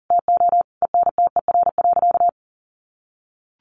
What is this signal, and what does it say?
Morse code, heard as a keyed tone, about 40 wpm, 700 Hz. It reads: TO ENTER.
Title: Sentences from Top 1300 Words 40wpm